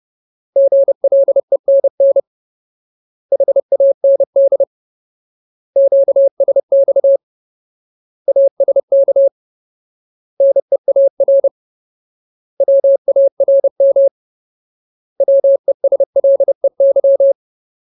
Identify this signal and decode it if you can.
Morse code, heard as a keyed tone, about 30 wpm, 550 Hz. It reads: GLENN HAND QSX ASK NEAR WARM WESLEY
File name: Ultimate Word List / 30WPM Ultimate